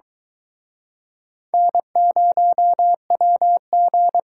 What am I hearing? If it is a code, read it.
Morse code, heard as a keyed tone, about 23 wpm, 700 Hz. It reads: N0WG